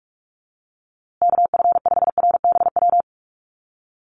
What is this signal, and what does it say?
Morse code, heard as a keyed tone, about 44 wpm, 690 Hz. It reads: KF5RBW
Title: USA Call Signs Repeated 3x 45wpm